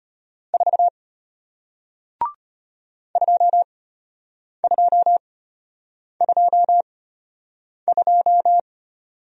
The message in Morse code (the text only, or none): 4 2222